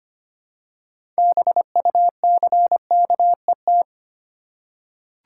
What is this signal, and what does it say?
Morse code, heard as a keyed tone, about 25 wpm, 690 Hz. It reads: BUCKET
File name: practice / 25wpm / Easy Words with Hints - 25wpm